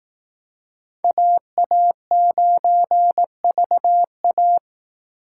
Morse code, heard as a keyed tone, about 18 wpm, 700 Hz. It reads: AA9VA
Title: Speed Racing - USA 2x2 Call Signs 15wpm